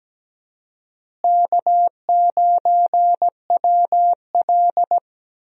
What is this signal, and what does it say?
Morse code, heard as a keyed tone, about 17 wpm, 700 Hz. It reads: K9WL
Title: USA Call Signs - 4 Characters - Repeated 2x - 17wpm